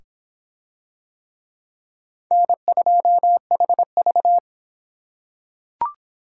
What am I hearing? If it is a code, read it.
Morse code, heard as a keyed tone, about 26 wpm, 700 Hz. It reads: N2HV